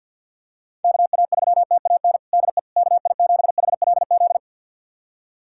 Morse code, heard as a keyed tone, about 48 wpm, 700 Hz. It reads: KA3TAN DE KI7HLZ
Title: USA Callsign DE Callsign - 6 Characters - 50wpm